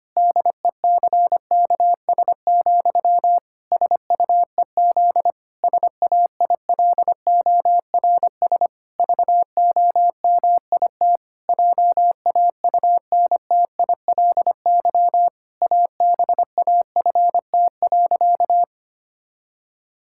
Morse code, heard as a keyed tone, about 25 wpm, 700 Hz. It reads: DECKS, SUEZ SAILORS VOMIT JAUNTILY ABAFT.
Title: Pangram Challenge v2 - 25wpm